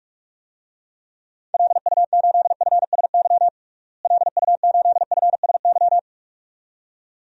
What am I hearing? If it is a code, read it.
Morse code, heard as a keyed tone, about 43 wpm, 700 Hz. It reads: LU8FSY LU8FSY
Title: DXCC-100 Call Signs - 6 Characters - Repeated 2x - 45wpm